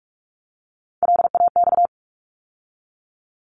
Morse code, heard as a keyed tone, about 44 wpm, 700 Hz. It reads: LAX
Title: ARRL Sections 45wpm